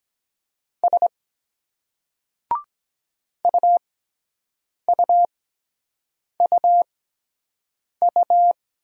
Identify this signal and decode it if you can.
Morse code, heard as a keyed tone, about 23 wpm, spaced as for 7 wpm, 700 Hz. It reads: S UUUU